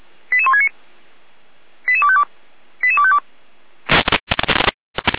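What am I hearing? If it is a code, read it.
Five-tone selective call (ZVEI), heard as quick runs of five tones, tones of about 70 ms. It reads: 88169 88252 88252
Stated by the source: ZVEI five-tone selective call 88166, received by SDR on the 455 channel.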